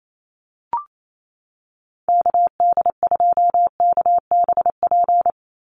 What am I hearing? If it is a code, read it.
Morse code, heard as a keyed tone, about 28 wpm, 700 Hz. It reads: KD2KBP
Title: USA Call Signs 28wpm